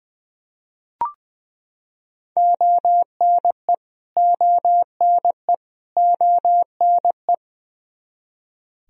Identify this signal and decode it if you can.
Morse code, heard as a keyed tone, about 20 wpm, 700 Hz. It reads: ONE ONE ONE
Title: Top 100 Words Repeated 20wpm